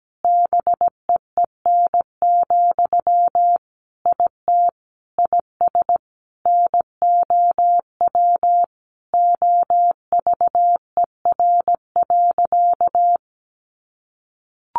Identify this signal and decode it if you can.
Morse code, heard as a keyed tone, about 17 wpm, 700 Hz. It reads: BEEN, IT IS NOW OVER.
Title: Sentences from Top 1100 Words 17wpm